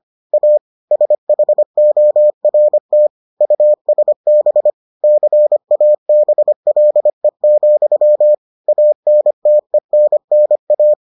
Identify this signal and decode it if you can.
Morse code, heard as a keyed tone, about 25 wpm, 600 Hz. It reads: A SHORT USB CABLE, ANTENNA